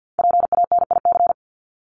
Morse code, heard as a keyed tone, about 50 wpm, 700 Hz. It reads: PANIC
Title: Top 1751-2000 Words 50wpm